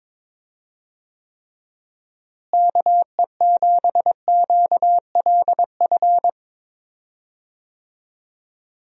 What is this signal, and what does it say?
Morse code, heard as a keyed tone, about 22 wpm, 700 Hz. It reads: KE7QLF
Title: USA Call Signs - 6 Characters - Repeated 2x - 22wpm